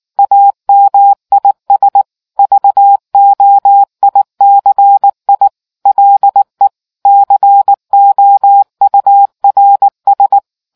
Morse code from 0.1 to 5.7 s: AMIS VOICI

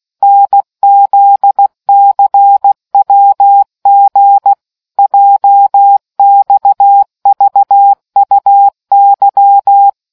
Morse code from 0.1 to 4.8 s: NZCWG

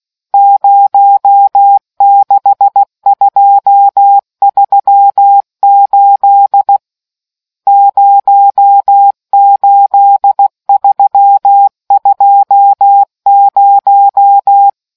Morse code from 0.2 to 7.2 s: 06238

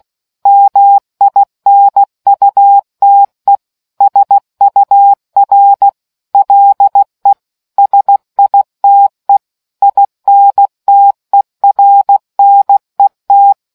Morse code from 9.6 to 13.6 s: INTERNET

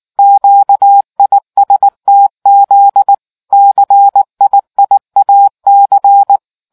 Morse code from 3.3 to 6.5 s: CIIAC

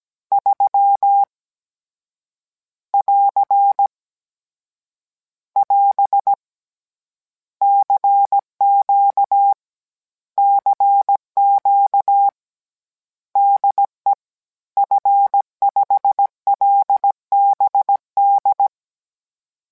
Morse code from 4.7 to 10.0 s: <AS> CQ

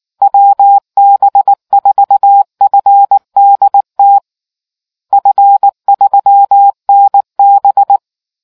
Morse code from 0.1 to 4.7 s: WB4FDT

